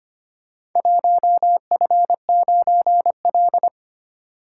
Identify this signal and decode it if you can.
Morse code, heard as a keyed tone, about 25 wpm, 700 Hz. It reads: 1F9L
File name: practice / 25wpm / Four Letter-Number Combinations - 25wpm